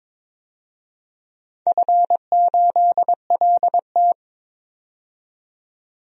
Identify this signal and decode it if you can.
Morse code, heard as a keyed tone, about 22 wpm, 700 Hz. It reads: F8LT
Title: DXCC-100 Call Signs - 4 Characters - 22wpm